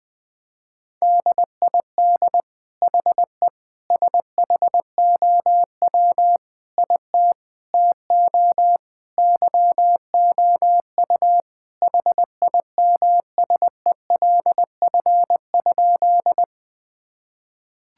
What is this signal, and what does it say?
Morse code, heard as a keyed tone, about 20 wpm, 700 Hz. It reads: DID HE SHOW IT TO YOU HIMSELF?
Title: Sentences from Top 1600 Words 20wpm